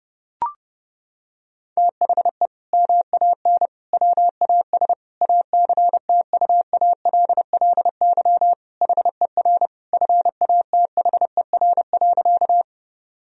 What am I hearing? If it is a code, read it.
Morse code, heard as a keyed tone, about 30 wpm, 700 Hz. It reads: THE MAN WAS ACTUALLY HER FATHER.
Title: Sentences from Top 1000 Words - Review Part 4 - 30wpm